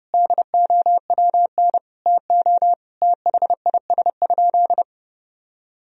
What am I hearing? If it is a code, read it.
Morse code, heard as a keyed tone, about 30 wpm, 700 Hz. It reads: DOWN TO THIS?
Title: Sentences from Top 1000 Words - Review Part 3 - 30wpm